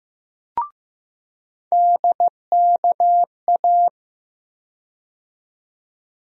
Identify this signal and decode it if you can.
Morse code, heard as a keyed tone, about 15 wpm, 700 Hz. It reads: DKA
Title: USA Call Sign Postfixes - 3 Letters 15wpm